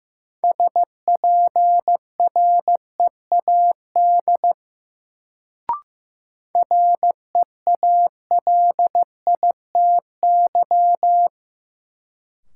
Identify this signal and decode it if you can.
Morse code, heard as a keyed tone, about 15 wpm, 700 Hz. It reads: SPREAD REALITY
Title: Sentences from Top 1200 Words - New Word Drill Alt - 15wpm